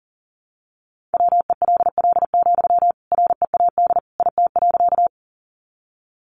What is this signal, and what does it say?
Morse code, heard as a keyed tone, about 40 wpm, 700 Hz. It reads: WELL, READ IT.
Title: Sentences from Top 200 Words 40wpm